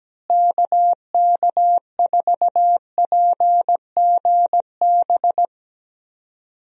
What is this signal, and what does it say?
Morse code, heard as a keyed tone, about 17 wpm, 700 Hz. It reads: KK4PGB
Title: USA Call Signs - 6 Characters - Repeated 2x - 17wpm